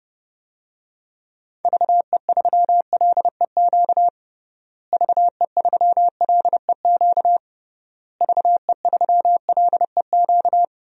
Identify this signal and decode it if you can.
Morse code, heard as a keyed tone, about 30 wpm, 700 Hz. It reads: VE3LEQ VE3LEQ VE3LEQ
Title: DXCC-100 Call Signs - 6 Characters - Repeated 3x - 30wpm